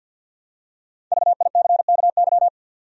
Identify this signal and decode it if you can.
Morse code, heard as a keyed tone, about 53 wpm, 700 Hz. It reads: VICKY
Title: USA Common Names - 50wpm